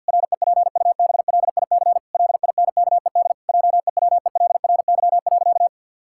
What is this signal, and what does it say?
Morse code, heard as a keyed tone, about 50 wpm, 700 Hz. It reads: REPUBLIC LINKED JEWELRY.
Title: Pangram Challenge v2 - 50wpm